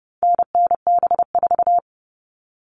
Morse code, heard as a keyed tone, about 30 wpm, 700 Hz. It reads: NNB4
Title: All Cut Numbers - 4-Digit 30wpm